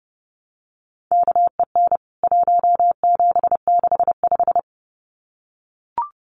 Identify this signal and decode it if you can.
Morse code, heard as a keyed tone, about 30 wpm, 700 Hz. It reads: KEN 1765
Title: CWT Exchange 30wpm